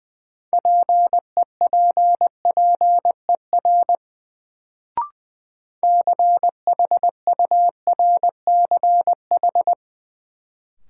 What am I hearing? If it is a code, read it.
Morse code, heard as a keyed tone, about 20 wpm, 700 Hz. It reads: PEPPER CHURCH
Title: Sets of 5 Words - New Word Drill - 20wpm